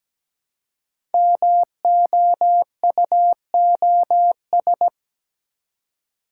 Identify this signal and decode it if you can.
Morse code, heard as a keyed tone, about 17 wpm, 700 Hz. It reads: MOUOS
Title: Code Groups - 5 Letters - 17wpm